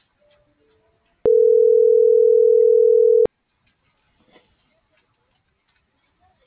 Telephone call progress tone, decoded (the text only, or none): ringback tone